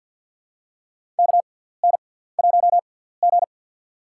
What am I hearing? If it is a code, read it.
Morse code, heard as a keyed tone, about 52 wpm, spaced as for 17 wpm, 700 Hz. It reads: KN1G
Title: USA Call Signs - 6x Character Spacing - 50wpm